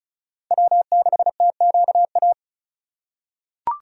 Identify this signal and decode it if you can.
Morse code, heard as a keyed tone, about 36 wpm, 700 Hz. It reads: W6TQA
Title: USA Call Signs - 4x Character Spacing - 35wpm